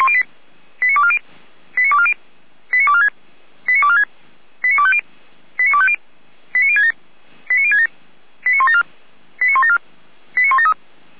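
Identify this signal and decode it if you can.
Five-tone selective call (ZVEI), heard as quick runs of five tones, tones of about 70 ms. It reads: 89240 89240 89246 89246 89250 89250 89067 89067 89174 89174 89173
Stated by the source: ZVEI five-tone call received by SDR, server of Feuerwehr Orlamünde fire brigade.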